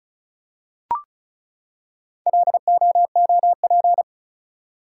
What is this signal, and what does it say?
Morse code, heard as a keyed tone, about 36 wpm, 700 Hz. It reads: LOOP
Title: Top 180 QSO Elements 35wpm